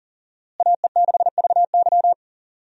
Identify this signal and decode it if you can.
Morse code, heard as a keyed tone, about 40 wpm, 700 Hz. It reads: AE6VY
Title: USA Call Signs - 5 Characters - Repeated 3x - 40wpm